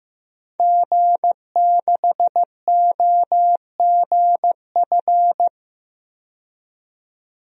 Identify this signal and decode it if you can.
Morse code, heard as a keyed tone, about 15 wpm, 700 Hz. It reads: G6OGF
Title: DXCC-100 Call Signs - 5 Characters - 15wpm